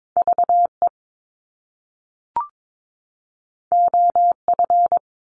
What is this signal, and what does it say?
Morse code, heard as a keyed tone, about 22 wpm, 700 Hz. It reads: VE OF
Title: Top Two Letter Combinations in Words 22wpm